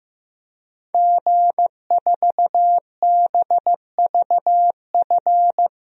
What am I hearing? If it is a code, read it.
Morse code, heard as a keyed tone, about 16 wpm, 700 Hz. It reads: G4BVF
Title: DXCC-100 Call Signs - 5 Characters - Repeated 3x - 15wpm